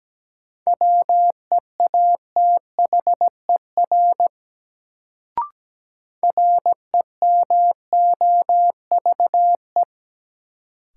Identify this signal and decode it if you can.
Morse code, heard as a keyed tone, about 17 wpm, 700 Hz. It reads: WEATHER REMOVE
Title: Sentences from Top 800 Words - New Word Drill - 17wpm